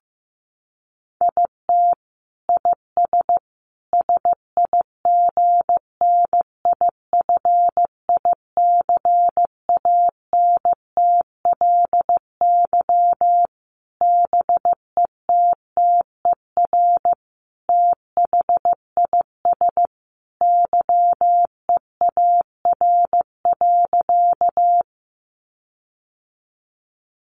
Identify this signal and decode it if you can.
Morse code, heard as a keyed tone, about 15 wpm, 700 Hz. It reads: IT IS SIGNIFICANTLY BETTER THIS YEAR.
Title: Sentences from Top 1500 Words 15wpm